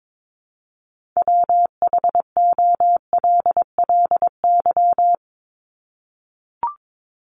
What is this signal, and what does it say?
Morse code, heard as a keyed tone, about 22 wpm, 700 Hz. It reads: WHOLLY
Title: LY Suffix - Top 100 Words with Base Words 22wpm